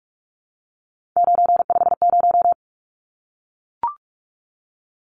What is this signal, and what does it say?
Morse code, heard as a keyed tone, about 44 wpm, 700 Hz. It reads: 950
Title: Numbers - 3-Digit 45wpm